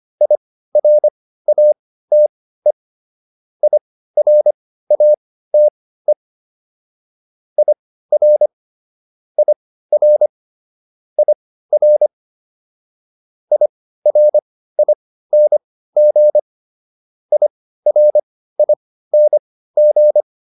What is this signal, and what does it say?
Morse code, heard as a keyed tone, about 25 wpm, spaced as for 15 wpm, 600 Hz. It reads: IRATE IRATE IR IR IR IRING IRING